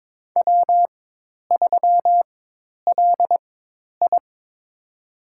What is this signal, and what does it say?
Morse code, heard as a keyed tone, about 22 wpm, spaced as for 10 wpm, 700 Hz. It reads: W3LI
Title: USA Call Signs - 4 Characters - 4x Character Spacing - 22wpm